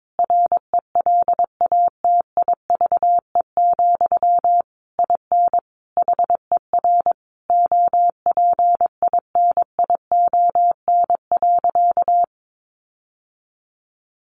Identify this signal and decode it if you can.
Morse code, heard as a keyed tone, about 22 wpm, 700 Hz. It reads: RELATIVE, IN HER OPINION.
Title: Sentences from Top 1400 Words 22wpm